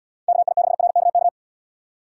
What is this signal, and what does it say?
Morse code, heard as a keyed tone, about 73 wpm, 700 Hz. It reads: KI6RKK